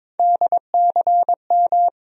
Morse code, heard as a keyed tone, about 22 wpm, 700 Hz. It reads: DCM
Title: USA Call Sign Postfixes - 3 Letters 22wpm